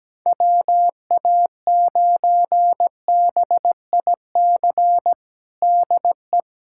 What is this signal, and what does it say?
Morse code, heard as a keyed tone, about 17 wpm, 700 Hz. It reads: WA9BIC DE
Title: USA Callsign DE Callsign 17wpm